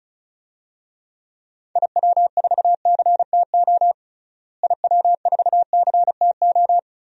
Morse code, heard as a keyed tone, about 34 wpm, 700 Hz. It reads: IW4CTO IW4CTO
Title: DXCC-100 Call Signs - 6 Characters - Repeated 3x - 35wpm